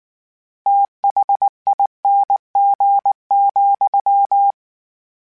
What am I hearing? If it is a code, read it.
Morse code, heard as a keyed tone, about 19 wpm, 800 Hz. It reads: THING,
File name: Special INGs / Special INGs-mp3s / VerySlowToFaster Ing @18